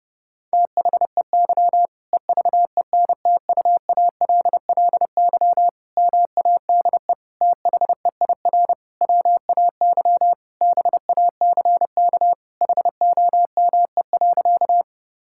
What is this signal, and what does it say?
Morse code, heard as a keyed tone, about 30 wpm, 700 Hz. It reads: THEY EVENTUALLY MADE THEIR WAY BACK HOME.
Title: Sentences from Top 1000 Words - Review Part 6 - 30wpm